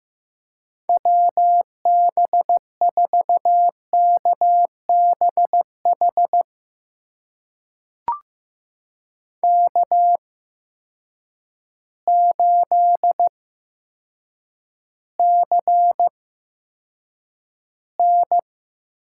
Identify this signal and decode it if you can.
Morse code, heard as a keyed tone, about 15 wpm, spaced as for 8 wpm, 700 Hz. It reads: WB4KBH K8CN
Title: USA Call Signs - 8x Character Spacing - 15wpm